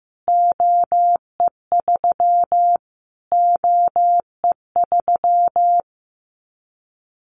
Morse code, heard as a keyed tone, about 14 wpm, 700 Hz. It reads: OE3 OE3
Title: Callsign Prefixes for Top 25 DXCC - Repeated 3x - 15wpm